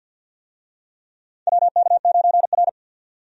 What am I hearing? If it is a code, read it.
Morse code, heard as a keyed tone, about 52 wpm, 700 Hz. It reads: WK9R
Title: USA Call Signs - 4 Characters - 50wpm